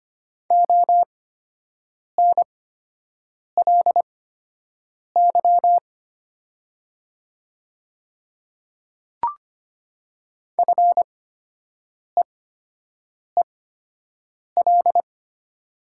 Morse code, heard as a keyed tone, about 25 wpm, spaced as for 7 wpm, 700 Hz. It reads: ONLY FEEL
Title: 8x Character Spacing - Top 500 Words - 25wpm